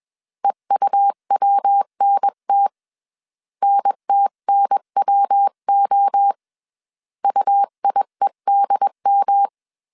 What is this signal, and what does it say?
Morse code, heard as a keyed tone, about 21 wpm, 800 Hz. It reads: EUWNT NTNWO UIEDM
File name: 20wpm